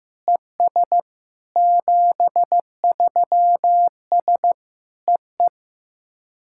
Morse code, heard as a keyed tone, about 15 wpm, 700 Hz. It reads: ES 73S EE
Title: SOTA QSO Practice 15wpm